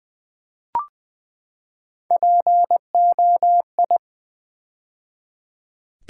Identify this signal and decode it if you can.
Morse code, heard as a keyed tone, about 20 wpm, 700 Hz. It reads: POI
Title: Build A Word - 3 Letters At A Time - 9-Letter Words - 20wpm